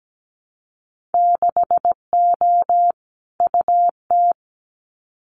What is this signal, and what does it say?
Morse code, heard as a keyed tone, about 17 wpm, 700 Hz. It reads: 6O UT
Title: Winter Field Day Exchanges Encore 17wpm